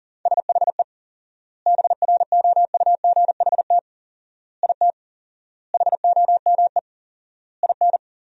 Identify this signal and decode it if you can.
Morse code, heard as a keyed tone, about 40 wpm, 700 Hz. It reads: SHE BROUGHT IT HOME IN
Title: 4x Word-Spacing - Sentences from Top 500 Words - 40wpm